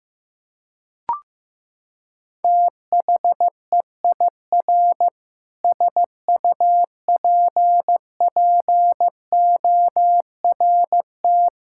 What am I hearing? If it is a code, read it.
Morse code, heard as a keyed tone, about 15 wpm, 700 Hz. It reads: THEIR SUPPORT
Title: Sentences from Top 1000 Words - Review Part 6 - 15wpm